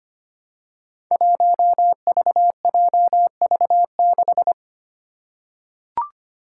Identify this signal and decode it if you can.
Morse code, heard as a keyed tone, about 25 wpm, 700 Hz. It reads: 1VJV6
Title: Five Challenging Letter-Number Combinations 25wpm